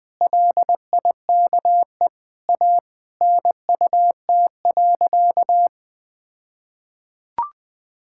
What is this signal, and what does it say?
Morse code, heard as a keyed tone, about 20 wpm, 700 Hz. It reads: LIKE A NUT.
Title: All-Age Appropriate Jokes - 20wpm